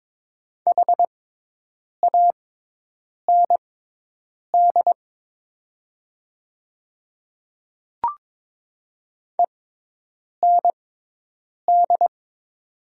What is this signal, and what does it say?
Morse code, heard as a keyed tone, about 22 wpm, spaced as for 8 wpm, 700 Hz. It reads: HAND END
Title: 6x Character Spacing - Top 200 Words - 22wpm